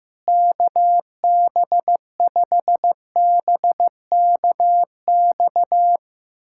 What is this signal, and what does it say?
Morse code, heard as a keyed tone, about 15 wpm, 700 Hz. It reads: KB5BKX
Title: USA Callsign DE Callsign 15wpm